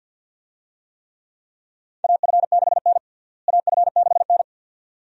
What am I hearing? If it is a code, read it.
Morse code, heard as a keyed tone, about 47 wpm, 700 Hz. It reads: AF6N AF6N
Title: USA Call Signs - 4 Characters - Repeated 2x - 50wpm